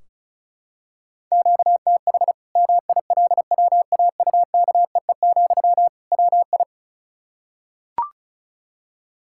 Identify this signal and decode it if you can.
Morse code, heard as a keyed tone, about 36 wpm, 700 Hz. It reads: QTH MILWAUKEE, WI
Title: Practice QSO - Exchange 1 - Beginning - 35wpm